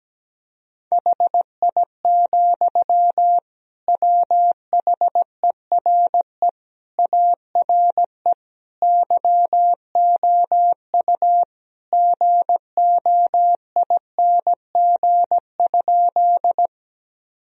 Morse code, heard as a keyed tone, about 17 wpm, 700 Hz. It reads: HI, WHERE ARE YOU GOING?
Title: Sentences from Top 1000 Words - Review Part 3 - 17wpm